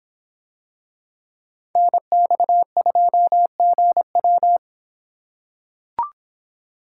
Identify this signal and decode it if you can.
Morse code, heard as a keyed tone, about 23 wpm, 700 Hz. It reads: NX2GW